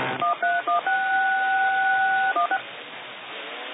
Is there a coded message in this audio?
Yes: DTMF from 0.2 to 2.6 s, heard as telephone touch tones, tones of about 140 ms. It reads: 131616